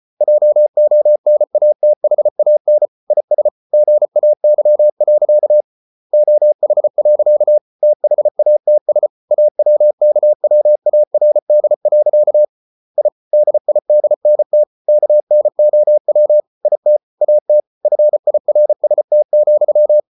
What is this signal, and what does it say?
Morse code, heard as a keyed tone, about 34 wpm, 600 Hz. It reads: JONATHAN IS GAY. OH. THATS AWKWARD. I DIDNT KNOW IT AT FIRST,